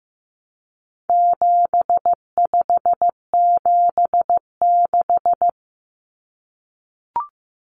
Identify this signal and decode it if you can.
Morse code, heard as a keyed tone, about 15 wpm, 700 Hz. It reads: 7576
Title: Common Cut Numbers - 4-Digit 15wpm